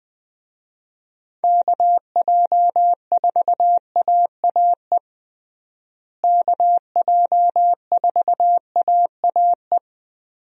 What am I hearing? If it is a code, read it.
Morse code, heard as a keyed tone, about 20 wpm, 700 Hz. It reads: KJ4AAE KJ4AAE
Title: USA Call Signs - 6 Characters - Repeated 2x - 20wpm